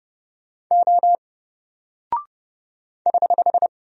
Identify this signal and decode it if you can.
Morse code, heard as a keyed tone, about 30 wpm, 700 Hz. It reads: O <HH>
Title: Single Character (Letter:Number:Punctuation:Prosign) 30wpm